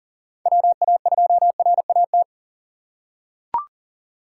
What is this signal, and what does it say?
Morse code, heard as a keyed tone, about 40 wpm, 700 Hz. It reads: WA2RAT
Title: USA Call Signs - 4x Character Spacing - 40wpm